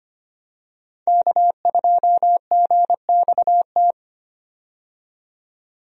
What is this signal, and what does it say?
Morse code, heard as a keyed tone, about 25 wpm, 690 Hz. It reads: K2GXT
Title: USA Call Signs - 5 Characters - 25wpm